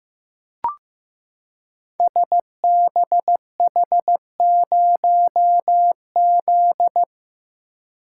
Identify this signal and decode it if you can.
Morse code, heard as a keyed tone, about 15 wpm, 700 Hz. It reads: SBH0Z
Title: Code Groups - 5 Letter:Numbers - 15wpm